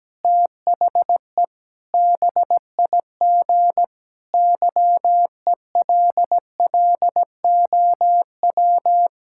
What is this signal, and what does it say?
Morse code, heard as a keyed tone, about 17 wpm, 700 Hz. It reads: THE BIG YELLOW
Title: Pangram Challenge v2 - 17wpm